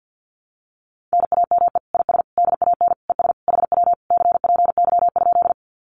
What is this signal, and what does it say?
Morse code, heard as a keyed tone, about 50 wpm, 700 Hz. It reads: NAME IS DAN ES HW CPY?
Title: Common QSO Phrases Encore 50wpm